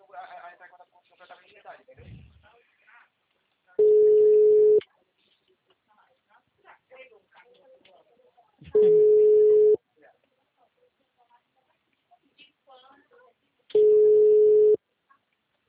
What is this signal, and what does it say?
Ringback tone (Europe, 425 Hz), with speech.